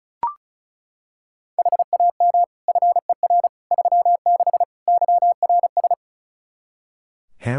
Morse code, heard as a keyed tone, about 36 wpm, 700 Hz. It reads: HAM FER 36 YRS
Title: Practice QSO - Exchange 3 - More Info - 35wpm